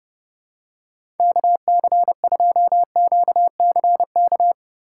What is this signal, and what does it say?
Morse code, heard as a keyed tone, about 31 wpm, 700 Hz. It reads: KC2QCK
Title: USA Call Signs - 6 Characters - 2x Character Spacing - 30wpm